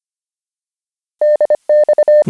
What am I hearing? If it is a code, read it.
Morse code, heard as a keyed tone, about 25 wpm, 600 Hz. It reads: DX